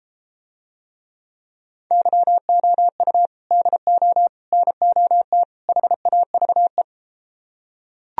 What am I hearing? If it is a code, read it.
Morse code, heard as a keyed tone, about 32 wpm, 700 Hz. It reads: YOU DO NOT HAVE